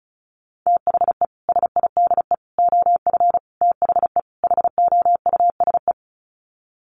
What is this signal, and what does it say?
Morse code, heard as a keyed tone, about 34 wpm, 700 Hz. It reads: THE SIDE OF THE HOUSE
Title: Sets of 5 Words Encore2 35wpm